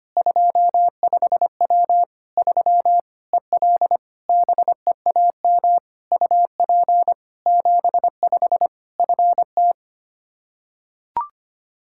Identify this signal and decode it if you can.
Morse code, heard as a keyed tone, about 25 wpm, 700 Hz. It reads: 25W 3 EL BEAM UP 75 FT
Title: K1USN SST QSO Practice 25wpm